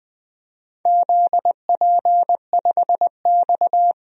Morse code, heard as a keyed tone, about 21 wpm, 700 Hz. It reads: ZP5X
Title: DXCC-100 Call Signs - 4 Characters - 20wpm